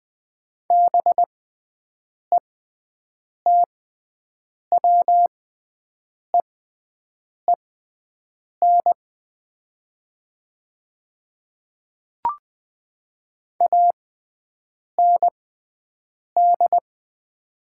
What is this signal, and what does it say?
Morse code, heard as a keyed tone, about 20 wpm, spaced as for 7 wpm, 690 Hz. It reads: BETWEEN AND
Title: 6x Character Spacing - Top 500 Words - 20wpm